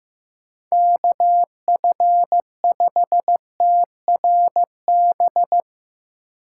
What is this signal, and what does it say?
Morse code, heard as a keyed tone, about 15 wpm, 700 Hz. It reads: KF5TRB
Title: USA Call Signs - 6 Characters - 2x Character Spacing - 15wpm